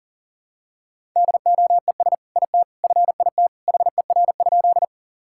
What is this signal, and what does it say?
Morse code, heard as a keyed tone, about 40 wpm, 700 Hz. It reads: DOES IT FIT HER?
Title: Sentences from Top 1000 Words - Review Part 5 - 40wpm